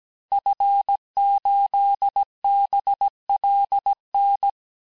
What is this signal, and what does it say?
Morse code, heard as a keyed tone, about 17 wpm, 800 Hz. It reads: F8BLN
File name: cours-du-lundi files / cours-1587-FIN-de-QSO-vitesse-16MPM